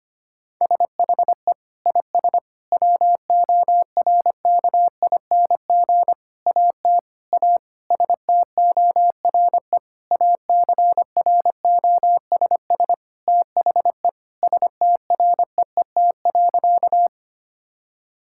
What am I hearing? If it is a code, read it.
Morse code, heard as a keyed tone, about 25 wpm, 700 Hz. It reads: SHE IS WORKING AT A STORE ACROSS THE STREET.
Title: Sentences from Top 1000 Words - Review Part 5 - 25wpm